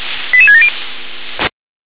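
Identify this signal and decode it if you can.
Five-tone selective call (ZVEI), heard as quick runs of five tones, tones of about 70 ms. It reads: 88670
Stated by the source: Five-tone ZVEI selective call 88786 received via SDR.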